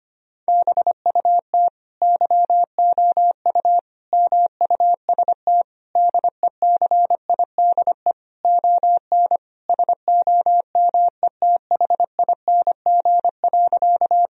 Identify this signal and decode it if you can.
Morse code, heard as a keyed tone, about 25 wpm, 700 Hz. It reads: BUT YOU MUST DECIDE ON SOMETHING.
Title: Sentences from Top 1000 Words - Review Part 3 - 25wpm